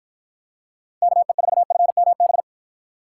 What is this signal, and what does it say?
Morse code, heard as a keyed tone, about 53 wpm, 700 Hz. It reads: KE4FMB